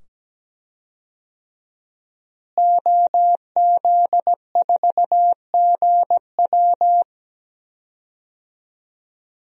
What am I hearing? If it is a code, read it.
Morse code, heard as a keyed tone, about 17 wpm, 700 Hz. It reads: OZ4GW
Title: DXCC-100 Call Signs - 5 Characters - Repeated 2x - 17wpm